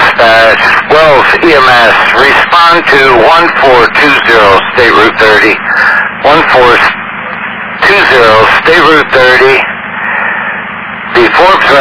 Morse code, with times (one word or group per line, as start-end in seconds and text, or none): none